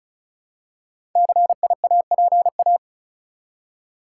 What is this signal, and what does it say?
Morse code, heard as a keyed tone, about 36 wpm, 700 Hz. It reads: CIAPA
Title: Code Groups - 5 Letters - 35wpm